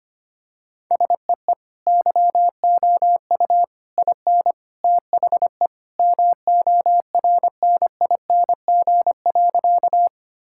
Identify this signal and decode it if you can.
Morse code, heard as a keyed tone, about 25 wpm, 700 Hz. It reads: SEE YOU IN THE MORNING.
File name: practice / 25wpm / Sentences with 3-Word Phrases - 25wpm